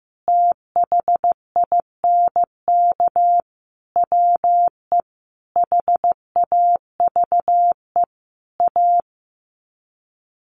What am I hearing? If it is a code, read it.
Morse code, heard as a keyed tone, about 15 wpm, 700 Hz. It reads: THINK WE HAVE A
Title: Sets of 5 Words Encore 15wpm